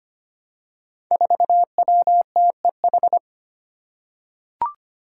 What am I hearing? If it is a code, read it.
Morse code, heard as a keyed tone, about 25 wpm, 700 Hz. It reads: 4WTEH
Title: Five Challenging Letter-Number Combinations 25wpm